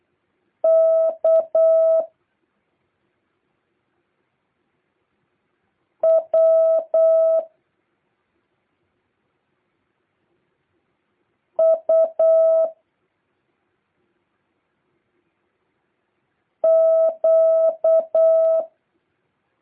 Morse code, heard as a keyed tone, about 8 wpm, 650 Hz. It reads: K W U Q
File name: jh8cbh / 2023cw